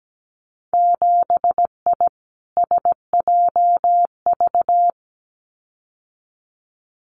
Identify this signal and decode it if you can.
Morse code, heard as a keyed tone, about 17 wpm, 700 Hz. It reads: 7I SJV
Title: Winter Field Day Exchanges Encore 17wpm